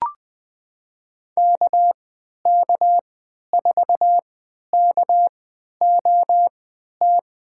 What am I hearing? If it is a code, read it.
Morse code, heard as a keyed tone, about 21 wpm, spaced as for 11 wpm, 700 Hz. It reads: KK4KOT
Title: USA Call Signs - 6 Characters - 3x Character Spacing - 20wpm